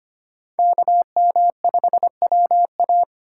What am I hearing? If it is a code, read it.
Morse code, heard as a keyed tone, about 25 wpm, 700 Hz. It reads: KM5WA
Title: USA Call Signs - 5 Characters - 4x Character Spacing - 25wpm